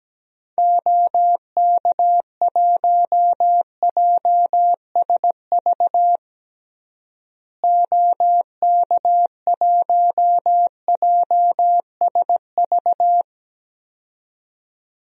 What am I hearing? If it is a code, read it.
Morse code, heard as a keyed tone, about 17 wpm, 700 Hz. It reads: OK1JSV OK1JSV
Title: DXCC-100 Call Signs - 6 Characters - Repeated 2x - 17wpm